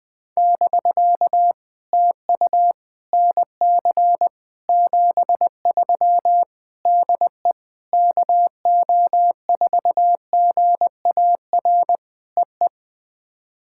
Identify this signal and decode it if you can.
Morse code, heard as a keyed tone, about 18 wpm, 700 Hz. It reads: <BK> TU NC 73 DE KO4GAR EE
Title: POTA QSO Practice - 20wpm